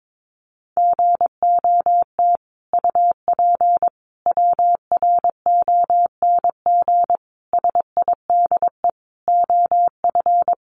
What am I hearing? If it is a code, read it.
Morse code, heard as a keyed tone, about 22 wpm, 700 Hz. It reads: GOT UP WRONG SIDE OF
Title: Sentences from Top 900 Words 22wpm